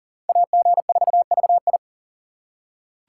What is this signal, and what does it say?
Morse code, heard as a keyed tone, about 40 wpm, 700 Hz. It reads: AG4VI
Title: USA Call Signs - 5 Characters - 3x Character Spacing - 40wpm